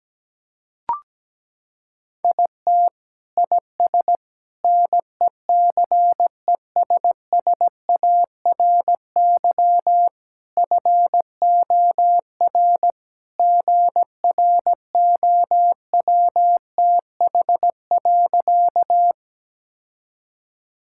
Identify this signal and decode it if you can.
Morse code, heard as a keyed tone, about 17 wpm, 700 Hz. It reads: IT IS NECESSARY FOR GROWTH.
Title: Sentences from Top 1000 Words - Review Part 5 - 17wpm